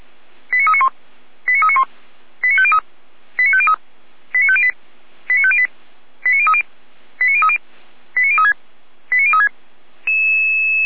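Five-tone selective call (ZVEI), heard as quick runs of five tones, tones of about 70 ms. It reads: 89391 89391 89503 89503 89508 89508 89030 89030 89036 89036
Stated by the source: ZVEI five-tone call received on an SDR.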